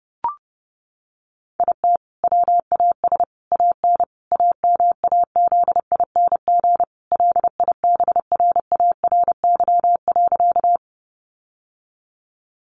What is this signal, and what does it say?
Morse code, heard as a keyed tone, about 30 wpm, 700 Hz. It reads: IT WAS AN AMAZING LIBRARY.
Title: Sentences from Top 1500 Words 30wpm